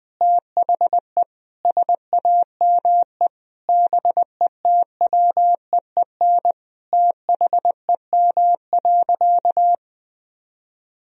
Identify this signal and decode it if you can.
Morse code, heard as a keyed tone, about 21 wpm, 700 Hz. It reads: THE SAME BETWEEN THEM.
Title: Sentences from Top 1000 Words - Review Part 1 - 20wpm